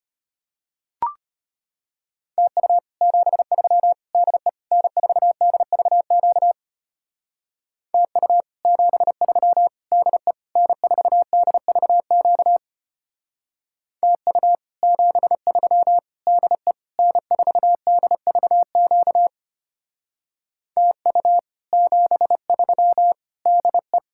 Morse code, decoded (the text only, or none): TU 73 DE N4DVQ TU 73 DE N4DVQ TU 73 DE N4DVQ TU 73 DE